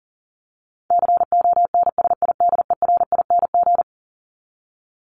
Morse code, heard as a keyed tone, about 37 wpm, 700 Hz. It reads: CONSIDERING
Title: ING Suffix - Top 100 Words with Base Words 40wpm